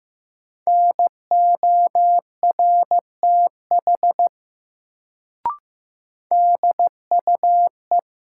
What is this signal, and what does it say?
Morse code, heard as a keyed tone, about 15 wpm, 700 Hz. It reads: NORTH DUE
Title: Sets of 5 Words Encore - New Word Drill - 15wpm